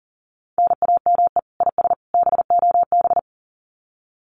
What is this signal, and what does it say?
Morse code, heard as a keyed tone, about 40 wpm, 700 Hz. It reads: NAME IS BOB
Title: Common QSO Phrases 40wpm